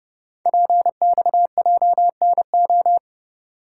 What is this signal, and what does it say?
Morse code, heard as a keyed tone, about 30 wpm, 690 Hz. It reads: PXJNO
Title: Code Groups - 5 Letters - 30wpm